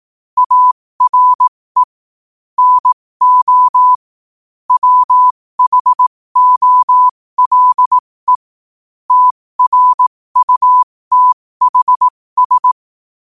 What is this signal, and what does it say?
Morse code, heard as a keyed tone, about 18 wpm, spaced as for 15 wpm, 1000 Hz. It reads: ARE NO WHOLE TRUTHS
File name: qotdfiles / 15wpm